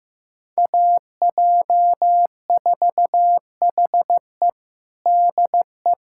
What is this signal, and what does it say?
Morse code, heard as a keyed tone, about 15 wpm, 700 Hz. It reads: AJ4HE DE
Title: USA Callsign DE Callsign - 5 Characters - 15wpm